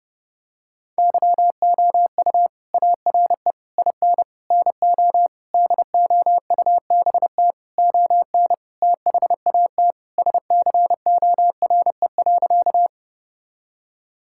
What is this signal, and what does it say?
Morse code, heard as a keyed tone, about 32 wpm, 700 Hz. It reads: YOU ARE IN NO DOUBT ON THAT SCORE.
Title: Sentences from Top 1600 Words 30wpm